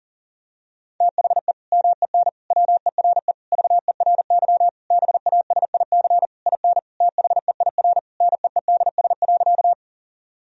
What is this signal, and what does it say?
Morse code, heard as a keyed tone, about 40 wpm, 700 Hz. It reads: THE MEN WERE VERY BASIC IN THEIR NEEDS.
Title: Sentences from Top 1000 Words - Review Part 3 - 40wpm